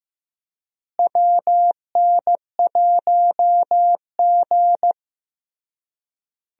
Morse code, heard as a keyed tone, about 16 wpm, 700 Hz. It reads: WN1G
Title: USA Call Signs - 4 Characters - Repeated 3x - 15wpm